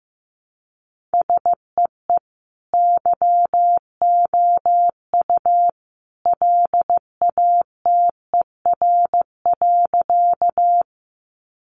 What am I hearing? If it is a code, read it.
Morse code, heard as a keyed tone, about 15 wpm, 700 Hz. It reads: SEE YOU LATER.
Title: Sentences from Top 700 Words 15wpm